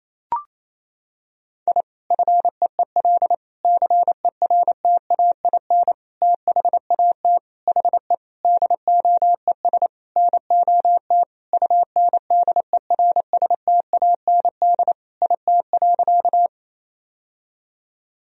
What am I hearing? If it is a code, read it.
Morse code, heard as a keyed tone, about 27 wpm, 700 Hz. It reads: I FEEL CERTAIN THAT HE DOES NOT UNDERSTAND IT.
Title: Sentences from Top 1000 Words - Review Part 2 - 28wpm